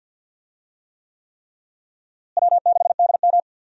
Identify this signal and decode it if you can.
Morse code, heard as a keyed tone, about 53 wpm, 700 Hz. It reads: W6DM
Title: USA Call Signs - 4 Characters - Repeated 3x - 50wpm